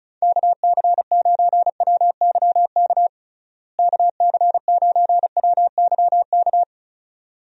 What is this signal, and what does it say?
Morse code, heard as a keyed tone, about 36 wpm, 700 Hz. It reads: KC9WYK KC9WYK
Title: USA Call Signs - 6 Characters - Repeated 2x - 35wpm